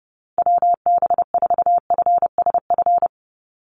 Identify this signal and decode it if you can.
Morse code, heard as a keyed tone, about 30 wpm, 700 Hz. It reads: WB4FSF
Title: Call Signs - 30wpm